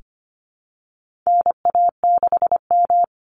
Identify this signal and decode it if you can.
Morse code, heard as a keyed tone, about 25 wpm, 700 Hz. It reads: NA6M
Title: Call Signs Repeated 3x - 25wpm